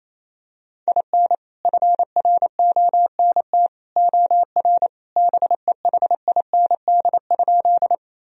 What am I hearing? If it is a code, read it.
Morse code, heard as a keyed tone, about 28 wpm, 700 Hz. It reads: IN FRONT OR BEHIND?
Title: Sentences from Top 1000 Words - Review Part 2 - 28wpm